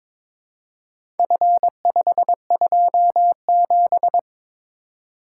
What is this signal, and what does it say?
Morse code, heard as a keyed tone, about 22 wpm, 700 Hz. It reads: F527
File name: practice / 22wpm / Four Letter-Number Combinations - 22wpm